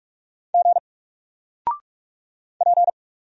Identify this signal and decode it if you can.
Morse code, heard as a keyed tone, about 45 wpm, 700 Hz. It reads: G P
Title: Single Character (Letter:Number:Punctuation:Prosign) 45wpm